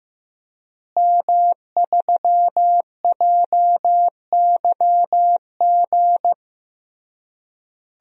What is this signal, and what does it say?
Morse code, heard as a keyed tone, about 14 wpm, 700 Hz. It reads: M3JYG
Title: DXCC-100 Call Signs - 5 Characters - Repeated 3x - 15wpm